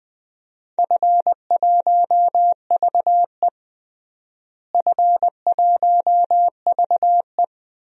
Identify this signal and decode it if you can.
Morse code, heard as a keyed tone, about 21 wpm, 700 Hz. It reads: F1VE F1VE
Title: DXCC-100 Call Signs - 4 Characters - Repeated 3x - 20wpm